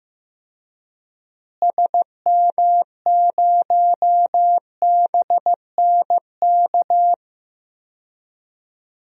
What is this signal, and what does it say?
Morse code, heard as a keyed tone, about 15 wpm, 700 Hz. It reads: SM0BNK
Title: DXCC-100 Call Signs - 6 Characters - Repeated 3x - 15wpm